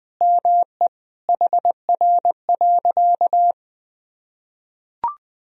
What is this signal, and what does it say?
Morse code, heard as a keyed tone, about 20 wpm, 700 Hz. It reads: ME HR.
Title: Sentences from Top 200 Words and Abbreviations 20wpm